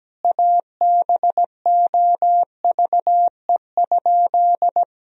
Morse code, heard as a keyed tone, about 17 wpm, 700 Hz. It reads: ABOVE?
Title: Sentences from Top 1000 Words - Review Part 2 - 17wpm